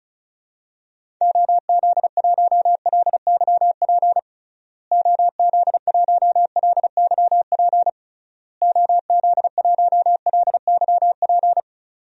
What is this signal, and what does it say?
Morse code, heard as a keyed tone, about 36 wpm, 700 Hz. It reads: OZ1LYP OZ1LYP OZ1LYP
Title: DXCC-100 Call Signs - 6 Characters - Repeated 3x - 35wpm